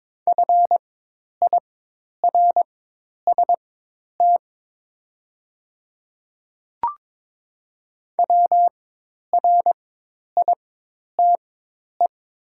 Morse code, heard as a keyed tone, about 22 wpm, spaced as for 10 wpm, 690 Hz. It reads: FIRST WRITE
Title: 4x Character Spacing - Top 100 Words - 22wpm